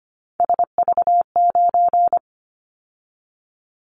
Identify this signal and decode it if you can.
Morse code, heard as a keyed tone, about 25 wpm, 700 Hz. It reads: SV9
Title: Callsign Prefixes for Top 100 DXCC - 25wpm